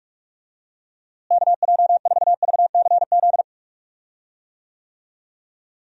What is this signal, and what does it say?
Morse code, heard as a keyed tone, about 46 wpm, 700 Hz. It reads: KJ4VCZ
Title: USA Call Signs - 6 Characters - Repeated 2x - 45wpm